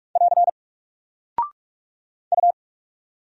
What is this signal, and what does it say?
Morse code, heard as a keyed tone, about 44 wpm, 700 Hz. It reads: <AR> U
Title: Single Character (Letter:Number:Punctuation:Prosign) 45wpm